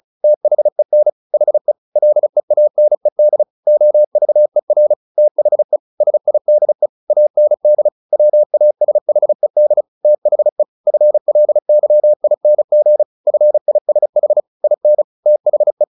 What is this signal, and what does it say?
Morse code, heard as a keyed tone, about 34 wpm, 600 Hz. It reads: THEN HE LEANED OVER THE SIDE AND WASHED THE FLYING FISH IN THE
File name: the old man and the sea 2 35wpm